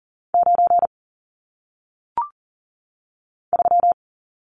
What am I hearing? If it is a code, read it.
Morse code, heard as a keyed tone, about 40 wpm, 700 Hz. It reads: 9 3
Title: Numbers - 1-Digit 40wpm